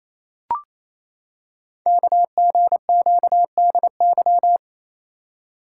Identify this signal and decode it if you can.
Morse code, heard as a keyed tone, about 27 wpm, 700 Hz. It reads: KGQDY
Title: Code Groups - 5 Letters - 28wpm